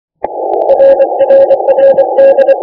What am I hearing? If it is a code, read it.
Morse code, heard as a keyed tone, about 24 wpm, 530 Hz, with noise. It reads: RRRD
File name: iw0ffk / 2009 07 03 JO6EDD IW0FFK rec by jo6edd